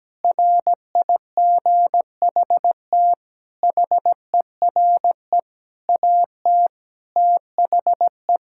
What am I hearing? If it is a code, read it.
Morse code, heard as a keyed tone, about 17 wpm, 700 Hz. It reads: RIGHT HERE AT THE
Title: Sentences from Top 1600 Words 17wpm